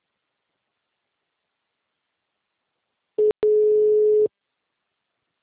Ringback tone (Europe, 425 Hz).
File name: out- -2010- - - .167